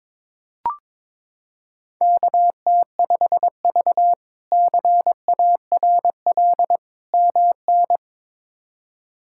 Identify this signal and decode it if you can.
Morse code, heard as a keyed tone, about 22 wpm, 700 Hz. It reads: KT5V CARL MN
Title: K1USN SST QSO Practice - 22wpm